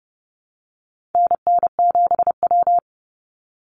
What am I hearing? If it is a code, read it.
Morse code, heard as a keyed tone, about 30 wpm, 700 Hz. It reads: NN7W
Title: Call Signs - 30wpm